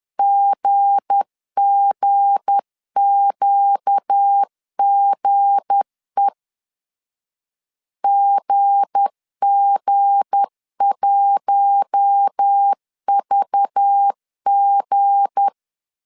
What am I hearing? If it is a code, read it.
Morse code, heard as a keyed tone, about 10 wpm, 800 Hz. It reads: GGQGE GG1VG
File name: LEZIONI / 7 GF9Z / 7 G 10wpm 2501